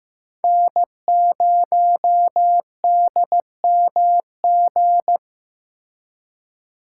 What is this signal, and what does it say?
Morse code, heard as a keyed tone, about 18 wpm, 700 Hz. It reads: N0DMG
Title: Speed Racing - USA Call Signs 15wpm